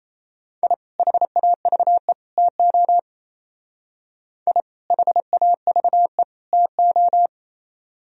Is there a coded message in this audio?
Yes: Morse code from 0.6 to 7.3 s, heard as a keyed tone, about 36 wpm, 700 Hz. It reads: I HAVE TO I HAVE TO